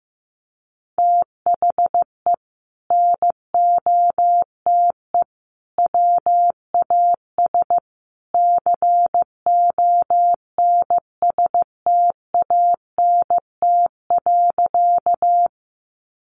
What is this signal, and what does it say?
Morse code, heard as a keyed tone, about 15 wpm, 700 Hz. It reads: THE NOTE WAS CONSTANT.
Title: Sentences from Top 1500 Words 15wpm